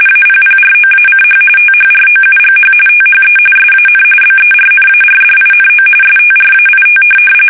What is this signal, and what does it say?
RTTY, about 45.45 baud. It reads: APPROACH THROUGH THE SAN BERNARDINO STRAIT
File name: fsk850 / USS-Massachusetts